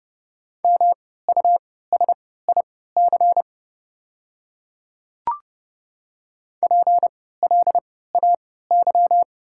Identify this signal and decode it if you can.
Morse code, heard as a keyed tone, about 30 wpm, spaced as for 17 wpm, 700 Hz. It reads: MUSIC PLAY
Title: 3x Character Spacing - Top 300 Words - 30wpm